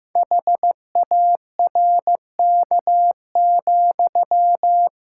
Morse code, heard as a keyed tone, about 15 wpm, 700 Hz. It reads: HARK,
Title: Pangram Challenge v2 - 15wpm